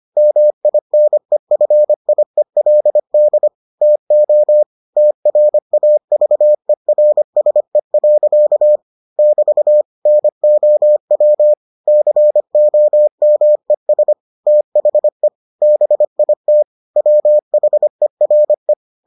Morse code, heard as a keyed tone, about 25 wpm, 600 Hz. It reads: MINEFIELD TO TRAVERSE. <BT> NOW COMES THE BIT WHERE